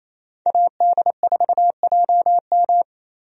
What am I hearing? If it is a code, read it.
Morse code, heard as a keyed tone, about 28 wpm, 700 Hz. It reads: AD4JM
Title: USA Call Signs - 5 Characters - Repeated 3x - 28wpm